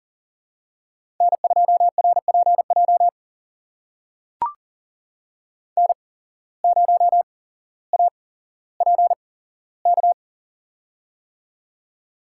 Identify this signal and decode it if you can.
Morse code, heard as a keyed tone, about 40 wpm, spaced as for 17 wpm, 700 Hz. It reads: N2RPJ N0APK
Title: USA Call Signs - 5 Characters - 8x Character Spacing - 40wpm